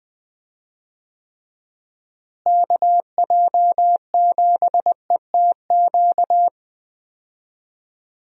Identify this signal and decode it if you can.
Morse code, heard as a keyed tone, about 20 wpm, 700 Hz. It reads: KJ7ETQ
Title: USA Call Signs - 6 Characters - Repeated 2x - 20wpm